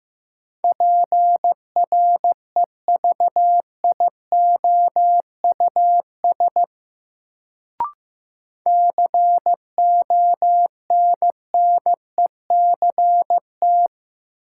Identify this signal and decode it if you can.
Morse code, heard as a keyed tone, about 15 wpm, 700 Hz. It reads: PREVIOUS CONNECT
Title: Sentences from Top 1500 Words - New Word Drill - 15wpm